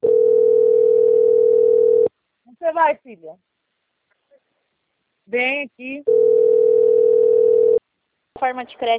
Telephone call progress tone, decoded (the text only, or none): ringback tone